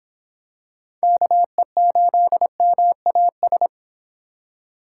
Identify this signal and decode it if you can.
Morse code, heard as a keyed tone, about 23 wpm, 700 Hz. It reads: KE8MAS